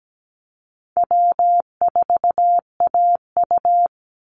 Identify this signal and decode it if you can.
Morse code, heard as a keyed tone, about 17 wpm, 700 Hz. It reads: W4AU
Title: Easy Call Signs Repeated 3x - 17wpm